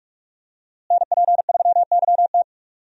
Morse code, heard as a keyed tone, about 46 wpm, 700 Hz. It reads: NP3YT
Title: USA Call Signs - 5 Characters - Repeated 2x - 45wpm